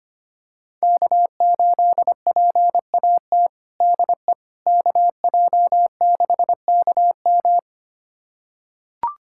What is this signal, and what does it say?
Morse code, heard as a keyed tone, about 25 wpm, 700 Hz. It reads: K8PAT DE KJ6KM
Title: USA Callsign DE Callsign - 5 Characters - 25wpm